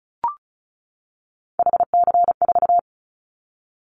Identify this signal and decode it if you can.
Morse code, heard as a keyed tone, about 34 wpm, 700 Hz. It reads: HC4
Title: Callsign Prefixes for Top 100 DXCC - 35wpm